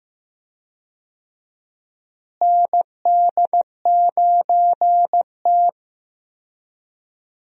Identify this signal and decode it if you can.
Morse code, heard as a keyed tone, about 15 wpm, 700 Hz. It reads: ND9T
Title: USA Call Signs - 4 Characters - Repeated 2x - 15wpm